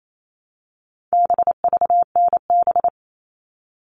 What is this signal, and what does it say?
Morse code, heard as a keyed tone, about 28 wpm, 700 Hz. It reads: BVNB
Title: All Cut Numbers - 4-Digit 28wpm